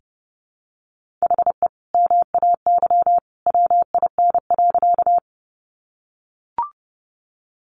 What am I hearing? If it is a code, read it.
Morse code, heard as a keyed tone, about 30 wpm, 700 Hz. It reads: HE MAY WIN.
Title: Sentences from Top 1500 Words 30wpm